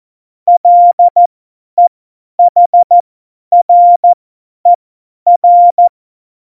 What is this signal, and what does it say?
Morse code, heard as a keyed tone, about 14 wpm, spaced as for 10 wpm, 700 Hz. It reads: LEHRER
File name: Morsekurs / Witze / Witz 04. Juni 2019, 50 F70